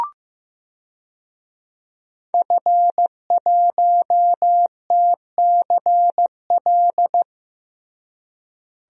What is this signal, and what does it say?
Morse code, heard as a keyed tone, about 16 wpm, 700 Hz. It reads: F1TCL
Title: DXCC-100 Call Signs - 5 Characters - Repeated 3x - 15wpm